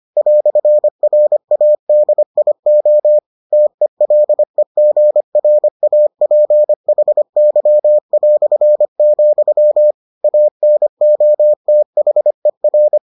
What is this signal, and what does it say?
Morse code, heard as a keyed tone, about 25 wpm, 600 Hz. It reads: "RADIO TELEGRAPHY", ANOTHER